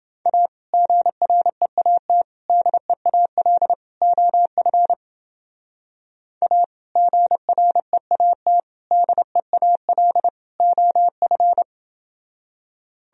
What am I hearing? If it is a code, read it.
Morse code, heard as a keyed tone, about 28 wpm, 700 Hz. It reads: A GREAT DEAL OF A GREAT DEAL OF